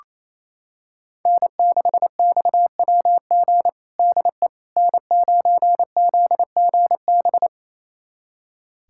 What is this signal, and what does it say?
Morse code, heard as a keyed tone, about 28 wpm, 700 Hz. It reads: N6XWG DE N9ZGB
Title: USA Callsign DE Callsign - 5 Characters - 28wpm